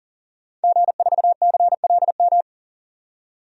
Morse code, heard as a keyed tone, about 40 wpm, 700 Hz. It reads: G4CLM
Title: DXCC-100 Call Signs - 5 Characters - 40wpm